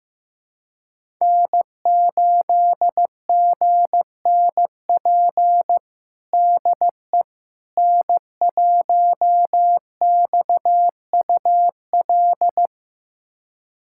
Morse code, heard as a keyed tone, about 15 wpm, 700 Hz. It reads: N8GNP DE N1XUL
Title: USA Callsign DE Callsign - 5 Characters - 15wpm